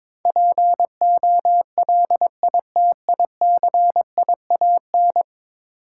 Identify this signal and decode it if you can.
Morse code, heard as a keyed tone, about 22 wpm, 700 Hz. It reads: POLITICIAN
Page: morse-code-ninja-practice-22wpm